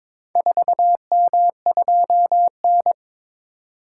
Morse code, heard as a keyed tone, about 22 wpm, 700 Hz. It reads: 4M2N
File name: practice / 22wpm / Four Letter-Number Combinations - 22wpm